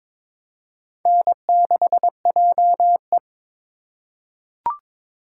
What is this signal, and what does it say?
Morse code, heard as a keyed tone, about 22 wpm, 700 Hz. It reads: N6JE
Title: USA Call Signs - 2x Character Spacing - 22wpm